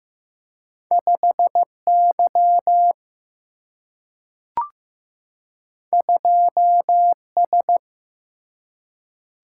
Morse code, heard as a keyed tone, about 15 wpm, 700 Hz. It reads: 5Y 2S
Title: Single-Letter Single-Number 15wpm